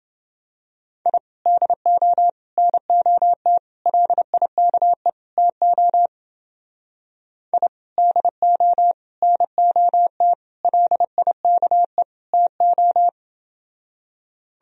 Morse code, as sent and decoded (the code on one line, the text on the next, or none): .. / -.. --- / -. --- - / .-.. .. -.- . / - --- / .. / -.. --- / -. --- - / .-.. .. -.- . / - ---
I DO NOT LIKE TO I DO NOT LIKE TO